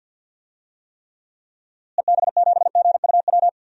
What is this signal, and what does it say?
Morse code, heard as a keyed tone, about 47 wpm, 700 Hz. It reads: EB7GUW
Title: DXCC-100 Call Signs - 6 Characters - Repeated 2x - 50wpm